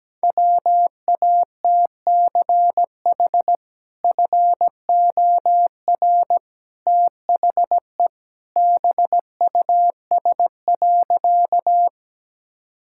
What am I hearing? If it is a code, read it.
Morse code, heard as a keyed tone, about 17 wpm, 700 Hz. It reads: WATCH FOR THE BUS.
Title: Sentences from Top 1000 Words - Review Part 6 - 17wpm